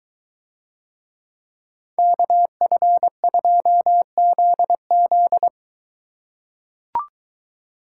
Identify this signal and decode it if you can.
Morse code, heard as a keyed tone, about 22 wpm, 700 Hz. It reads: KF2ZZ